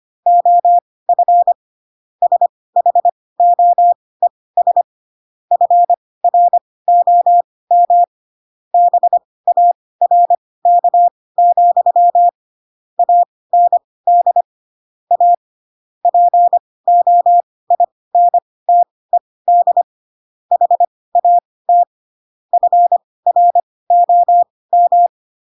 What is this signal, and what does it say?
Morse code, heard as a keyed tone, about 25 wpm, spaced as for 18 wpm, 700 Hz. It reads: OF SHOES FROM BARK, AND A POINTED HAT FROM